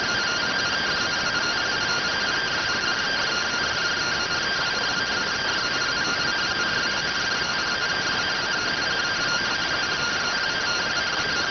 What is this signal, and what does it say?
Two-tone teleprinter signal, about 170 Hz shift.